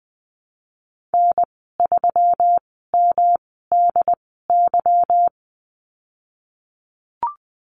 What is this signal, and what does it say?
Morse code, heard as a keyed tone, about 20 wpm, spaced as for 14 wpm, 700 Hz. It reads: N3MDY
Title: 154 Lesson 25g - Letter p - Call Signs 20wpm